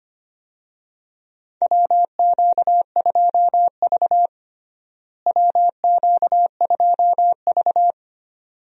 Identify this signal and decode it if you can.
Morse code, heard as a keyed tone, about 25 wpm, 700 Hz. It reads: WQ2V WQ2V
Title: USA Call Signs - 4 Characters - Repeated 2x - 25wpm